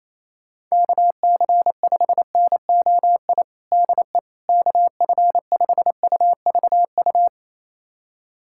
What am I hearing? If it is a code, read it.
Morse code, heard as a keyed tone, about 28 wpm, 700 Hz. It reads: KC5NOI DE KF5UVU
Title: USA Callsign DE Callsign - 6 Characters - 28wpm